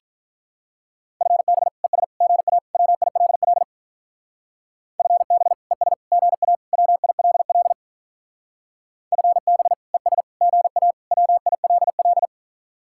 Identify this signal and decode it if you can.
Morse code, heard as a keyed tone, about 42 wpm, 700 Hz. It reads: FB ES GA WILL FB ES GA WILL FB ES GA WILL
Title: Speed-Racing Common QSO Phrases - 35wpm